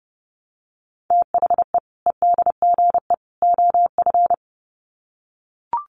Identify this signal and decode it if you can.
Morse code, heard as a keyed tone, about 30 wpm, 700 Hz. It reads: THE EDGE OF
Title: Sets of 3 Words 30wpm